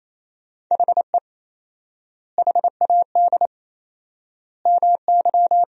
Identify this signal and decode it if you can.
Morse code, heard as a keyed tone, about 28 wpm, 700 Hz. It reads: HE HAD MY
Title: 4x Word-Spacing - Sentences from Top 400 Words - 28wpm